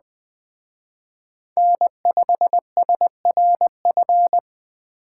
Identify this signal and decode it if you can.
Morse code, heard as a keyed tone, about 20 wpm, 700 Hz. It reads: N5SRF
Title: Speed Racing - USA Call Signs 17wpm